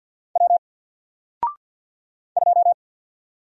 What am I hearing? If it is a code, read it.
Morse code, heard as a keyed tone, about 50 wpm, 700 Hz. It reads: W 2
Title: Single Character (Letter:Number:Punctuation:Prosign) Alt 50wpm